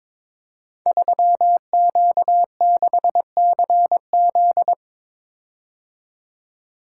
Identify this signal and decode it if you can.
Morse code, heard as a keyed tone, about 22 wpm, 700 Hz. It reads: 3Q6CZ
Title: Code Groups - 5 Letter:Numbers - 22wpm